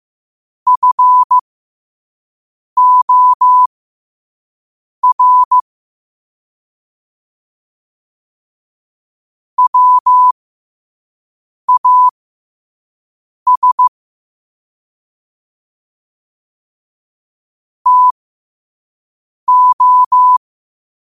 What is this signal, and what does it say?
Morse code, heard as a keyed tone, about 15 wpm, spaced as for 5 wpm, 1000 Hz. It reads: FOR WAS TO
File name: qotdfiles / 5wpm